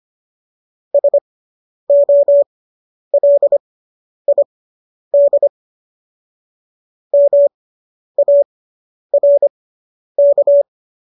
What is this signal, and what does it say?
Morse code, heard as a keyed tone, about 25 wpm, spaced as for 10 wpm, 570 Hz. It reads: SOLID MARK